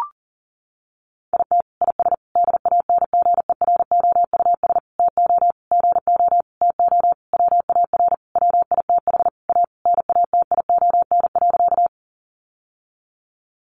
Morse code, heard as a keyed tone, about 40 wpm, 700 Hz. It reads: IT IS DANGEROUS TO GO TO WAR WITH A NATION.
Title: Sentences from Top 1200 Words 40wpm